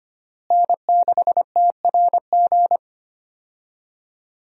Morse code, heard as a keyed tone, about 25 wpm, 700 Hz. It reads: N6TRG
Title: USA Call Signs - 5 Characters - Repeated 3x - 25wpm